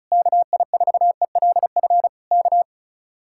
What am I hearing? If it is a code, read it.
Morse code, heard as a keyed tone, about 34 wpm, 700 Hz. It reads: KI4ELF K
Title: Speed-Racing Common QSO Phrases - 35wpm